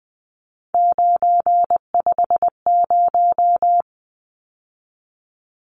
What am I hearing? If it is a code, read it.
Morse code, heard as a keyed tone, about 20 wpm, 700 Hz. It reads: 950
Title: Numbers - 3-Digit 20wpm